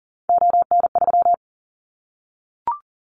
Morse code, heard as a keyed tone, about 40 wpm, 700 Hz. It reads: ON3
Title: Callsign Prefixes for Top 25 DXCC - Repeated 3x - 40wpm